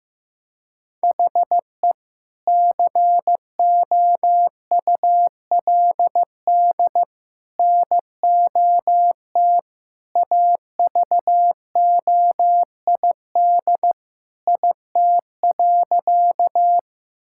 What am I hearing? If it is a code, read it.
Morse code, heard as a keyed tone, about 15 wpm, 700 Hz. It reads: HE COULD NOT AVOID IT.
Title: Sentences from Top 1000 Words - Review Part 6 - 15wpm